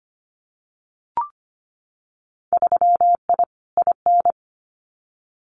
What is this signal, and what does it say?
Morse code, heard as a keyed tone, about 25 wpm, 700 Hz. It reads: 3I IN
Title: Winter Field Day Exchanges 25wpm